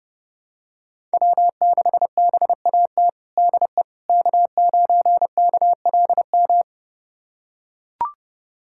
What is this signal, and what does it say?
Morse code, heard as a keyed tone, about 28 wpm, 700 Hz. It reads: W6BAT DE K9KLM
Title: USA Callsign DE Callsign - 5 Characters - 30wpm